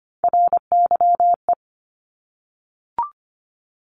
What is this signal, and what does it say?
Morse code, heard as a keyed tone, about 25 wpm, 700 Hz. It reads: RYE
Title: Multi-letter Phonemes Repeated 3x 25wpm